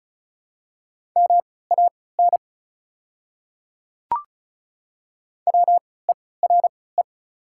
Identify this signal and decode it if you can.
Morse code, heard as a keyed tone, about 36 wpm, spaced as for 20 wpm, 700 Hz. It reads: MAN WERE
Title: 3x Character Spacing - Top 500 Words - 35wpm